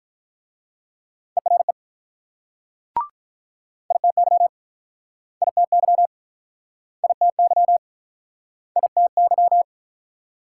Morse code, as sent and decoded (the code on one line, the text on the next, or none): . .-. / .. - -.-- / .. - -.-- / .. - -.-- / .. - -.--
ERE ITY ITY ITY ITY